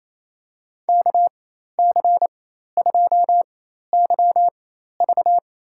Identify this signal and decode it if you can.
Morse code, heard as a keyed tone, about 28 wpm, spaced as for 13 wpm, 700 Hz. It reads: KC2YV
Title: USA Call Signs - 4x Character Spacing - 28wpm